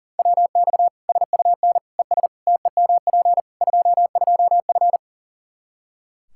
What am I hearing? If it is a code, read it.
Morse code, heard as a keyed tone, about 40 wpm, 700 Hz. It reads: WX SUN ES TEMP 22F
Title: Practice QSO - Exchange 2 - Info - 40wpm